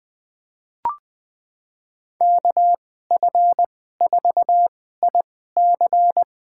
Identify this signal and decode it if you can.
Morse code, heard as a keyed tone, about 20 wpm, spaced as for 14 wpm, 700 Hz. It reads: KF4IC
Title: USA Call Signs - 5 Characters - 2x Character Spacing - 20wpm